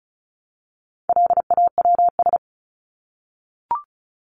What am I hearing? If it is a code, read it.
Morse code, heard as a keyed tone, about 34 wpm, 700 Hz. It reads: LAWS
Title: ES:S Suffix - Top 100 Words with Base Words 35wpm